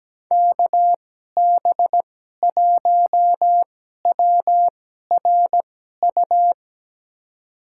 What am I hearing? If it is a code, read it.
Morse code, heard as a keyed tone, about 17 wpm, spaced as for 12 wpm, 700 Hz. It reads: KB1WRU
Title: USA Call Signs - 6 Characters - 2x Character Spacing - 17wpm